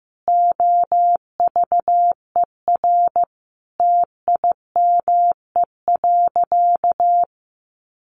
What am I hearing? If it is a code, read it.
Morse code, heard as a keyed tone, about 15 wpm, 700 Hz. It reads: OVER TIME.
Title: Sentences from Top 1200 Words 15wpm